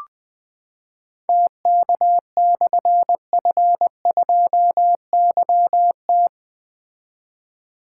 Morse code, heard as a keyed tone, about 20 wpm, 700 Hz. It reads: TK/F2YT
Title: Difficult Call Signs 20wpm